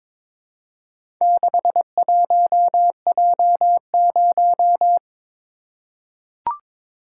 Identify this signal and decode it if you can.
Morse code, heard as a keyed tone, about 21 wpm, 700 Hz. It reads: 61J0
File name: practice / 22wpm / Four Letter-Number Combinations - 22wpm